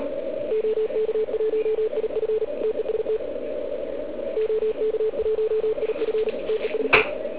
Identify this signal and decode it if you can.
Morse code, heard as a keyed tone, about 39 wpm, 440 Hz, with radio noise. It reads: OK1IF TEST OK1IF TEST